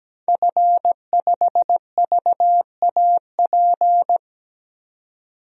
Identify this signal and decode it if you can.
Morse code, heard as a keyed tone, about 17 wpm, 700 Hz. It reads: F5VAP
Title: DXCC-100 Call Signs - 5 Characters - 17wpm